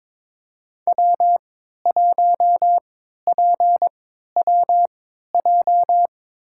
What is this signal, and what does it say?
Morse code, heard as a keyed tone, about 22 wpm, spaced as for 13 wpm, 700 Hz. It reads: W1PWJ
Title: USA Call Signs - 5 Characters - 3x Character Spacing - 22wpm